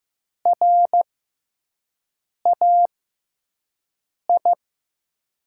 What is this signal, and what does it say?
Morse code, heard as a keyed tone, about 15 wpm, spaced as for 5 wpm, 700 Hz. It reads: RAI
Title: USA Call Signs - 5 Characters - 6x Character Spacing - 15wpm